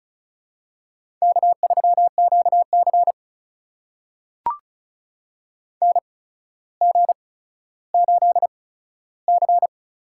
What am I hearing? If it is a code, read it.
Morse code, heard as a keyed tone, about 36 wpm, spaced as for 15 wpm, 700 Hz. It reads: K3QC NG8C
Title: USA Call Signs - 4 Characters - 8x Character Spacing - 35wpm